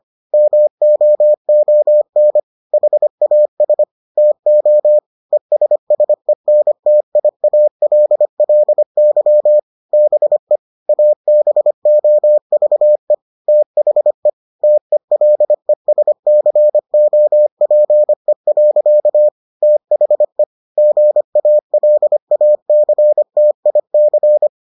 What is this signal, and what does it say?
Morse code, heard as a keyed tone, about 25 wpm, 600 Hz. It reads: MOON HAS TO ESSENTIALLY BE ABOVE THE TELESCOPE. THE GALACTIC